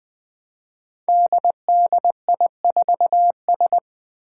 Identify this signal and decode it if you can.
Morse code, heard as a keyed tone, about 20 wpm, 700 Hz. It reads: DDI4S
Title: Five Challenging Letter-Number Combinations 20wpm